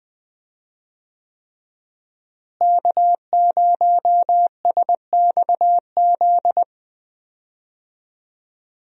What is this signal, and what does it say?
Morse code, heard as a keyed tone, about 21 wpm, 700 Hz. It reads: K0SXZ
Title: USA Call Signs - 5 Characters - Repeated 3x - 20wpm